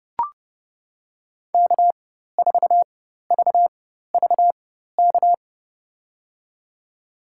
Morse code, heard as a keyed tone, about 31 wpm, spaced as for 14 wpm, 700 Hz. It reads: K4VVK
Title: USA Call Signs - 5 Characters - 4x Character Spacing - 30wpm